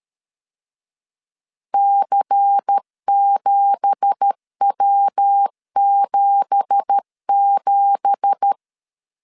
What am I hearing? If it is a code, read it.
Morse code, heard as a keyed tone, about 13 wpm, 800 Hz. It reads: C7W77
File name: LEZIONI / 9 2867 12wpm 2322